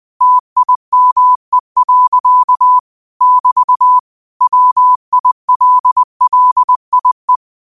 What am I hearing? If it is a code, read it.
Morse code, heard as a keyed tone, about 20 wpm, 1000 Hz. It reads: TIME. <BT> WILLIE
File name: qotdfiles / 20wpm